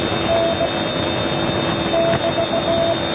Morse code, heard as a keyed tone, about 17 wpm, 670 Hz, with noise. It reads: N <BT>